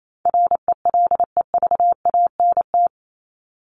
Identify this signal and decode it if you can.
Morse code, heard as a keyed tone, about 28 wpm, 700 Hz. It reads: RELEVANT
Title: Top 1001-1250 Words 28wpm